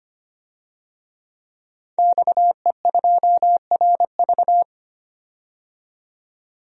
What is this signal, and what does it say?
Morse code, heard as a keyed tone, about 25 wpm, 700 Hz. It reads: XE2RV
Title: DXCC-100 Call Signs - 5 Characters - 25wpm